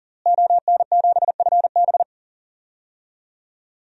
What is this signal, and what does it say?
Morse code, heard as a keyed tone, about 40 wpm, 700 Hz. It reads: ON7FB
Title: DXCC-100 Call Signs - 5 Characters - Repeated 3x - 40wpm